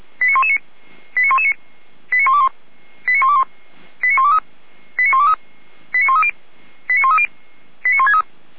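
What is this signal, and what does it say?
Five-tone selective call (ZVEI), heard as quick runs of five tones, tones of about 80 ms. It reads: 89119 89119 89121 89121 89123 89123 89130 89130 89163
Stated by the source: ZVEI five-tone selective call 89010 received on an SDR by Feuerwehr Orlamünde.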